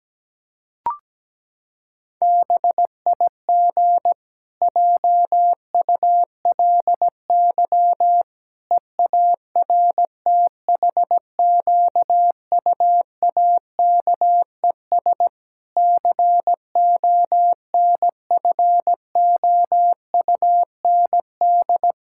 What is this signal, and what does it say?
Morse code, heard as a keyed tone, about 17 wpm, 700 Hz. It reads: BIG JULY EARTHQUAKES CONFOUND
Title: Pangram Challenge v2 - 17wpm